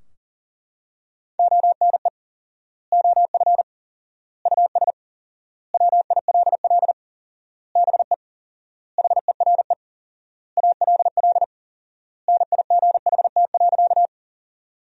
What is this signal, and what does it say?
Morse code, heard as a keyed tone, about 40 wpm, 700 Hz. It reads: ONE OF US WILL BE HERE ALL NIGHT.
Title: 4x Word-Spacing - Sentences from Top 300 Words - 40wpm